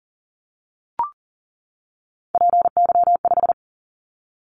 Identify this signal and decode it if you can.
Morse code, heard as a keyed tone, about 42 wpm, 690 Hz. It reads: PY5
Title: Callsign Prefixes for Top 100 DXCC - 40wpm